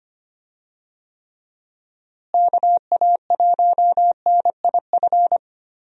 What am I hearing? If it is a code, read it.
Morse code, heard as a keyed tone, about 25 wpm, 700 Hz. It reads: KA1NIF
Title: USA Call Signs - 6 Characters - Repeated 3x - 25wpm